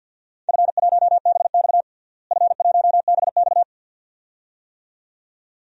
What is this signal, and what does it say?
Morse code, heard as a keyed tone, about 52 wpm, 700 Hz. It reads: F1BX F1BX
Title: DXCC-100 Call Signs - 4 Characters - Repeated 2x - 50wpm